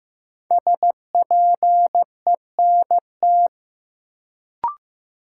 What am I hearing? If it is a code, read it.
Morse code, heard as a keyed tone, about 15 wpm, 700 Hz. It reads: SPENT
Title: Sets of 5 Words - New Word Drill - 15wpm